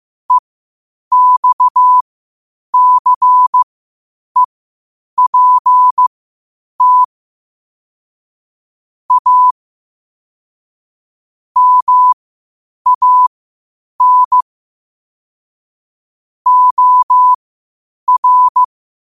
Morse code, heard as a keyed tone, about 15 wpm, spaced as for 8 wpm, 1000 Hz. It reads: EXCEPT A MAN OR
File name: qotdfiles / 8wpm